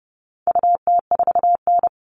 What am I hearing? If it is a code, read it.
Morse code, heard as a keyed tone, about 30 wpm, 700 Hz. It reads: UT4N
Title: All Cut Numbers - 4-Digit 30wpm